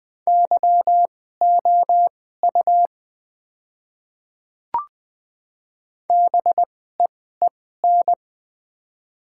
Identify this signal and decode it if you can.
Morse code, heard as a keyed tone, about 20 wpm, spaced as for 14 wpm, 700 Hz. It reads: YOU BEEN
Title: 2x Character Spacing - Top 100 Words - 20wpm